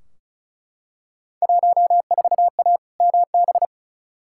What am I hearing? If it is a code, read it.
Morse code, heard as a keyed tone, about 36 wpm, 690 Hz. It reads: 14A MB
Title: Field Day Exchanges Encore - 35wpm